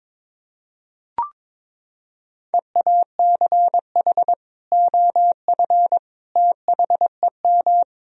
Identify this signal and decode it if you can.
Morse code, heard as a keyed tone, about 22 wpm, 700 Hz. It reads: EACH OF THEM
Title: Sentences from Top 1000 Words - Review Part 1 - 22wpm